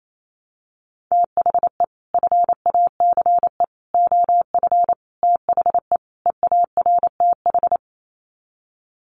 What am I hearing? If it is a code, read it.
Morse code, heard as a keyed tone, about 28 wpm, 700 Hz. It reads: THE FACE OF THE EARTH
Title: Sets of 5 Words Encore 28wpm